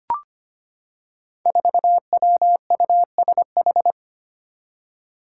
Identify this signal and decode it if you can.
Morse code, heard as a keyed tone, about 25 wpm, 700 Hz. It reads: 4WUSH
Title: Five Challenging Letter-Number Combinations 25wpm